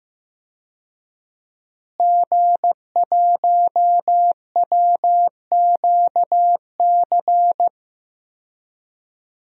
Morse code, heard as a keyed tone, about 15 wpm, 700 Hz. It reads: G1WQC
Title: DXCC-100 Call Signs - 5 Characters - Repeated 2x - 15wpm